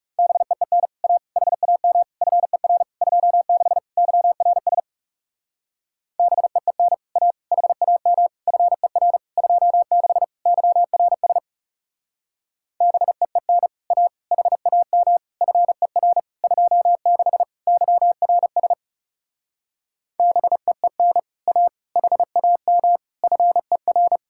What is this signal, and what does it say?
Morse code, heard as a keyed tone, about 40 wpm, 700 Hz. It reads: BEEN A HAM FER 26 YRS BEEN A HAM FER 26 YRS BEEN A HAM FER 26 YRS BEEN A HAM FER